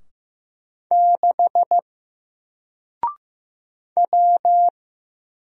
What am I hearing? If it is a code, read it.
Morse code, heard as a keyed tone, about 15 wpm, 700 Hz. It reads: 6 W
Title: Single Character (Letter:Number:Punctuation:Prosign) 15wpm